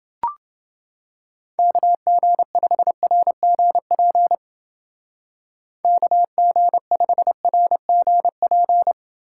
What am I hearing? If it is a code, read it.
Morse code, heard as a keyed tone, about 29 wpm, 700 Hz. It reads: KG5RGP KG5RGP